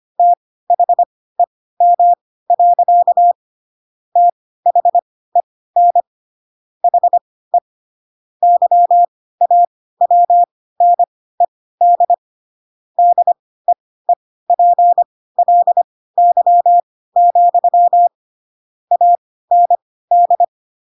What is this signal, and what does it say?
Morse code, heard as a keyed tone, about 25 wpm, spaced as for 16 wpm, 700 Hz. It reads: THEM. THEN HE YAWNED DEEPLY, AND
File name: Pinocchio / Pinocchio ch7 10 25 ik1hge